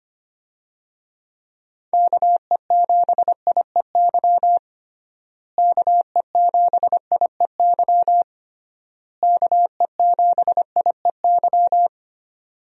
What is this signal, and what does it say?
Morse code, heard as a keyed tone, about 25 wpm, 700 Hz. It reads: KE7IEY KE7IEY KE7IEY
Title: USA Call Signs - 6 Characters - Repeated 3x - 25wpm